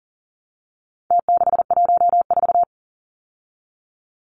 Morse code, heard as a keyed tone, about 40 wpm, 700 Hz. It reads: T614
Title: Numbers - Contest Style - 4-Digit 40wpm